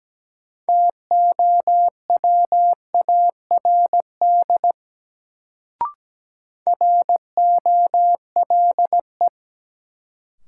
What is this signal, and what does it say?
Morse code, heard as a keyed tone, about 17 wpm, 700 Hz. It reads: TOWARD ROLE
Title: Sentences from Top 1000 Words - Review Part 6 - New Word Drill - 17wpm